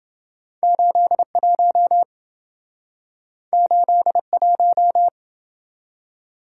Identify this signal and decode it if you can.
Morse code, heard as a keyed tone, about 28 wpm, 700 Hz. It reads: 81 81